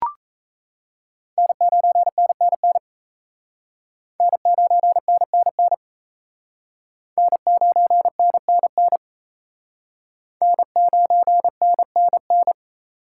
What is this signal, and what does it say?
Morse code, heard as a keyed tone, about 35 wpm, 700 Hz. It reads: N9NNN N9NNN N9NNN N9NNN
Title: Speed Racing - USA 1x3 Call Signs 28wpm